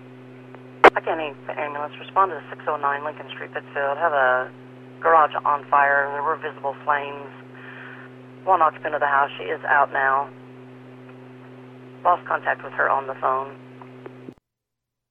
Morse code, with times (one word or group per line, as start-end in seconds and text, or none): none